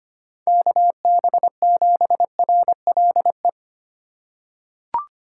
Morse code, heard as a keyed tone, about 25 wpm, 700 Hz. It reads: KB7RLE
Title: USA Call Signs - 4x Character Spacing - 25wpm